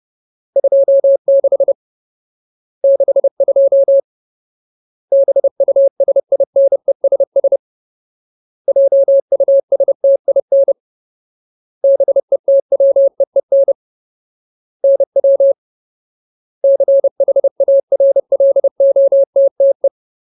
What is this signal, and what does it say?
Morse code, heard as a keyed tone, about 30 wpm, 550 Hz. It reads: 26 62 BUSINESS JUSTIN BETWEEN NW CHARLOTTE
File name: Ultimate Word List / 30WPM Ultimate